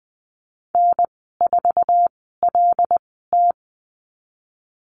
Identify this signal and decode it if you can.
Morse code, heard as a keyed tone, about 20 wpm, spaced as for 14 wpm, 700 Hz. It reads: N4LT
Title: 082 Lesson 14g - Number 2 - Call Signs 20wpm